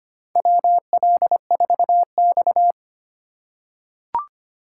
Morse code, heard as a keyed tone, about 25 wpm, 700 Hz. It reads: WL4X
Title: USA Call Signs - 4 Characters - 2x Character Spacing - 25wpm